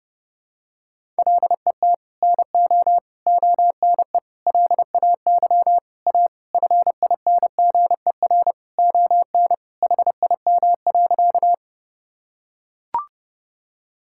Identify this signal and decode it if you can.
Morse code, heard as a keyed tone, about 30 wpm, 700 Hz. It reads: LET NO ONE LAY A FINGER ON HIM.
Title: Sentences from Top 1600 Words 30wpm